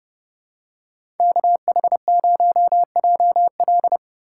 Morse code, heard as a keyed tone, about 30 wpm, 700 Hz. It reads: KH0JL
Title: USA Call Signs - 5 Characters - 3x Character Spacing - 30wpm